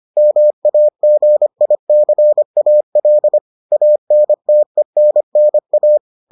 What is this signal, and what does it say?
Morse code, heard as a keyed tone, about 25 wpm, 600 Hz. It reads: MAGICAL ANTENNA